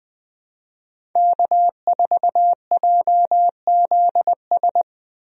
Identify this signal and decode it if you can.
Morse code, heard as a keyed tone, about 20 wpm, 700 Hz. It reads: K4JZS
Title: USA Call Signs - 5 Characters - 20wpm